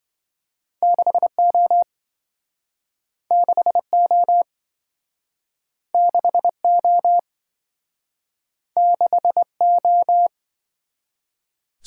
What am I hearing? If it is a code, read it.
Morse code, heard as a keyed tone, about 27 wpm, 700 Hz. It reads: 6O 6O 6O 6O